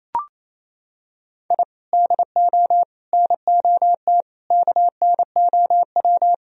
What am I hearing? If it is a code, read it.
Morse code, heard as a keyed tone, about 28 wpm, 700 Hz. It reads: I DO NOT KNOW
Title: Sentences from Top 1000 Words - Review Part 1 - 28wpm